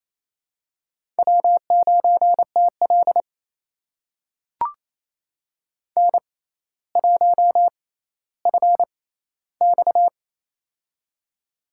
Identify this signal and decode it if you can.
Morse code, heard as a keyed tone, about 28 wpm, spaced as for 14 wpm, 700 Hz. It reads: W9TL N1FX
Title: USA Call Signs - 4 Characters - 6x Character Spacing - 28wpm